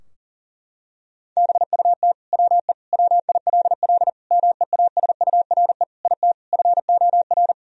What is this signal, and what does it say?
Morse code, heard as a keyed tone, about 40 wpm, 700 Hz. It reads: BUT WE WILL MEASURE IT FOR
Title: Sentences from Top 1000 Words - Review Part 2 - 40wpm